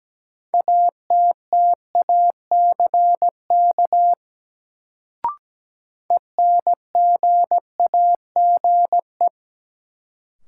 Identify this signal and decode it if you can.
Morse code, heard as a keyed tone, about 17 wpm, 700 Hz. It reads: ATTACK ENGAGE
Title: Sentences from Top 1500 Words - New Word Drill - 17wpm